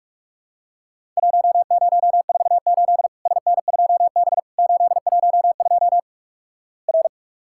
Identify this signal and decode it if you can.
Morse code, heard as a keyed tone, about 48 wpm, 700 Hz. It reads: 1048 SN2B 812 R
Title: Worked All Europe DX Contest - QTC Exchange - 45wpm